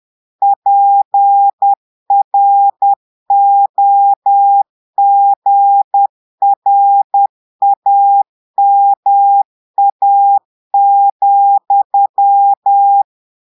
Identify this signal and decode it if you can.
Morse code, heard as a keyed tone, about 10 wpm, 800 Hz. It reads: PROGRAMA,
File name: portugais texte en clair 2 vitesse 10 mots minutes